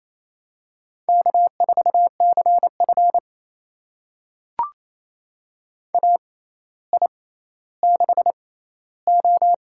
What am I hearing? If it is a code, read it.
Morse code, heard as a keyed tone, about 28 wpm, spaced as for 14 wpm, 700 Hz. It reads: K4CF AI6O
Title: USA Call Signs - 6x Character Spacing - 28wpm